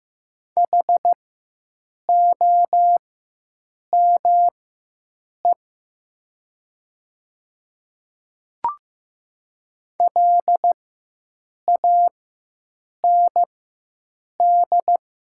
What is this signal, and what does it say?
Morse code, heard as a keyed tone, about 15 wpm, spaced as for 7 wpm, 700 Hz. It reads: HOME LAND
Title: 4x Character Spacing - Top 200 Words - 15wpm